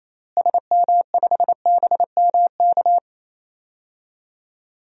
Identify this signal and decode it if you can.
Morse code, heard as a keyed tone, about 28 wpm, 700 Hz. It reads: SM5BMK
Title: DXCC-100 Call Signs - 6 Characters - Repeated 2x - 28wpm